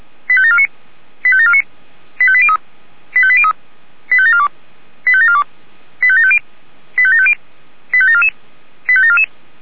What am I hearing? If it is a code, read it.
Five-tone selective call (ZVEI), heard as quick runs of five tones, tones of about 70 ms. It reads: 85749 85749 85803 85803 85742 85742 85750 85750 85744 85744